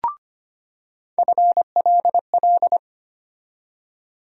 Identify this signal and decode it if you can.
Morse code, heard as a keyed tone, about 25 wpm, 700 Hz. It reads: FLL
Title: USA Call Sign Postfixes - 3 Letters 25wpm